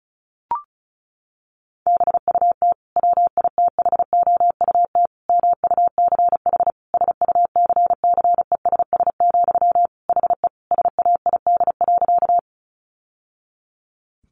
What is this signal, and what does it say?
Morse code, heard as a keyed tone, about 34 wpm, 700 Hz. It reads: BUT WITHOUT MUCH SUCCESS, HE SAID.
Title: Sentences from Top 900 Words 35wpm